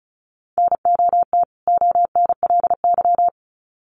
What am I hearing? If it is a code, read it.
Morse code, heard as a keyed tone, about 34 wpm, 700 Hz. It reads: NOT ONLY
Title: Sets of 2 Words 35wpm